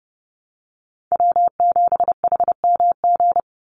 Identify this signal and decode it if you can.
Morse code, heard as a keyed tone, about 30 wpm, 700 Hz. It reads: W7HMG
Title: USA Call Signs 30wpm